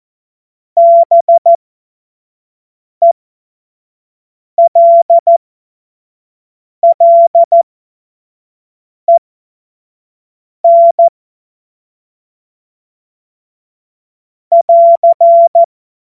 Morse code, heard as a keyed tone, about 14 wpm, spaced as for 5 wpm, 680 Hz. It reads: BELLEN <AR>
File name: Witz-12-August-2025-25-F70